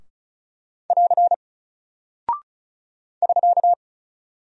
Morse code, heard as a keyed tone, about 36 wpm, 700 Hz. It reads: <AR> <SK>
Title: Single Character (Letter:Number:Punctuation:Prosign) Alt 35wpm